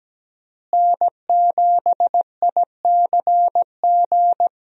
Morse code, heard as a keyed tone, about 17 wpm, 700 Hz. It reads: N7ICG
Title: USA Call Signs - 5 Characters - Repeated 2x - 17wpm